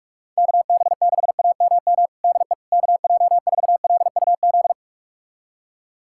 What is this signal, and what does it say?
Morse code, heard as a keyed tone, about 46 wpm, 700 Hz. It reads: KB6AMM DE KJ4LUZ
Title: USA Callsign DE Callsign - 6 Characters - 45wpm